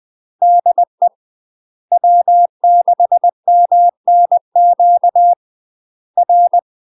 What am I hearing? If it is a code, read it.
Morse code, heard as a keyed tone, about 20 wpm, 700 Hz. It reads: DE W6MNQ R
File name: UK MASTER QSOs TRAINING / 20WPM / UK MASTER QSOs 0058